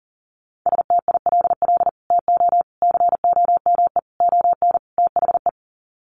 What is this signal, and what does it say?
Morse code, heard as a keyed tone, about 37 wpm, 700 Hz. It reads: STILL TO COME ON THE
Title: Sets of 5 Words Encore2 40wpm